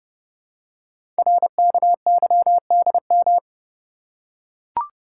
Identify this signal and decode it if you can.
Morse code, heard as a keyed tone, about 30 wpm, 700 Hz. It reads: RKYDM
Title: Code Groups - 5 Letters - 30wpm